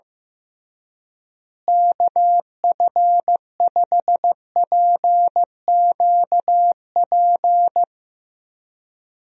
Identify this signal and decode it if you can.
Morse code, heard as a keyed tone, about 15 wpm, 700 Hz. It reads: KF5PQP
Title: USA Call Signs - 6 Characters - Repeated 2x - 15wpm